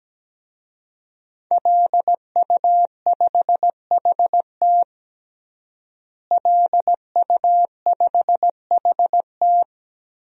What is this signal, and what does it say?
Morse code, heard as a keyed tone, about 17 wpm, 700 Hz. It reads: LU5HT LU5HT
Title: DXCC-100 Call Signs - 5 Characters - Repeated 2x - 17wpm